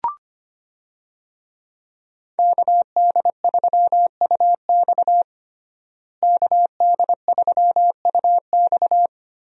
Morse code, heard as a keyed tone, about 25 wpm, 700 Hz. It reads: KD3UX KD3UX
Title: USA Call Signs - 5 Characters - Repeated 3x - 25wpm